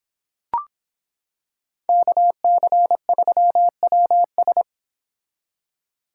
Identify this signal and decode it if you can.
Morse code, heard as a keyed tone, about 25 wpm, 700 Hz. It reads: KC3WS